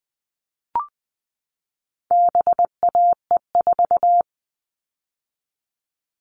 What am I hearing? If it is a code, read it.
Morse code, heard as a keyed tone, about 20 wpm, 700 Hz. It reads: BAE4
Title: All Cut Numbers - 4-Digit 20wpm